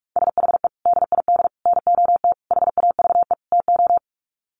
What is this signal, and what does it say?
Morse code, heard as a keyed tone, about 44 wpm, 700 Hz. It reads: SHE DID NOT HAVE TO
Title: Sets of 5 Words Encore2 45wpm